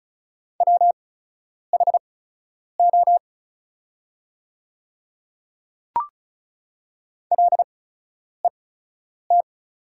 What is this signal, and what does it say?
Morse code, heard as a keyed tone, about 36 wpm, spaced as for 10 wpm, 700 Hz. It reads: WHO LET
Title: 8x Character Spacing - Top 300 Words - 35wpm